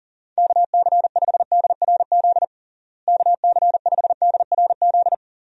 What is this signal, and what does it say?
Morse code, heard as a keyed tone, about 40 wpm, 700 Hz. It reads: KC5DRZ KC5DRZ
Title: USA Call Signs - 6 Characters - Repeated 2x - 40wpm